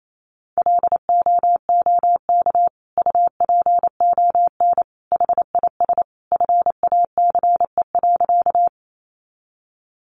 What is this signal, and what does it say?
Morse code, heard as a keyed tone, about 28 wpm, 700 Hz. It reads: LOOK UPON HIS FACE.
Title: Sentences from Top 1100 Words 28wpm